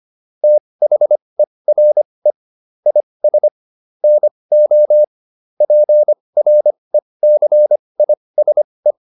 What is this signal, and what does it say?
Morse code, heard as a keyed tone, about 25 wpm, spaced as for 20 wpm, 600 Hz. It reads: THERE IS NO PRECISE